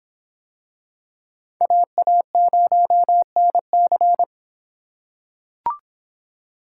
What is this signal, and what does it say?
Morse code, heard as a keyed tone, about 24 wpm, 700 Hz. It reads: AA0NC